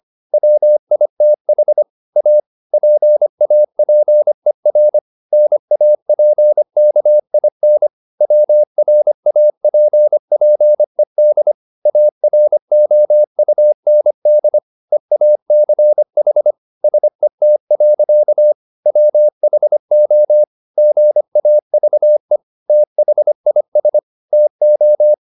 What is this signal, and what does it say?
Morse code, heard as a keyed tone, about 25 wpm, 600 Hz. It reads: WITH A PAPER NAPKIN WRAPPED AROUND EACH SET. WHO GAVE THIS TO
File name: the old man and the sea part 1 25wpm